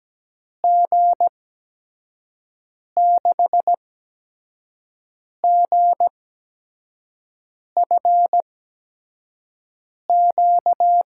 Morse code, heard as a keyed tone, about 17 wpm, spaced as for 5 wpm, 700 Hz. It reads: G6GFQ
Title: USA Call Signs - 6 Characters - 8x Character Spacing - 17wpm